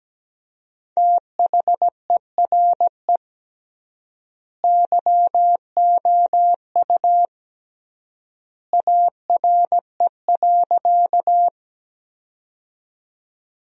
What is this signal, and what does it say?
Morse code, heard as a keyed tone, about 17 wpm, 700 Hz. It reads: THERE YOU ARE.
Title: 3x Word-Spacing - Sentences from Top 100 Words - 17wpm